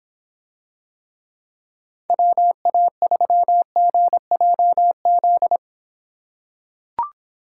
Morse code, heard as a keyed tone, about 26 wpm, 700 Hz. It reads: WA3GJZ